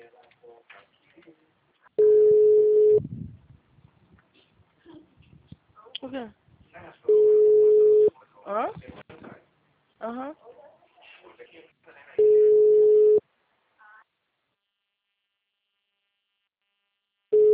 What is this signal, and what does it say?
Ringback tone (Europe, 425 Hz), with speech.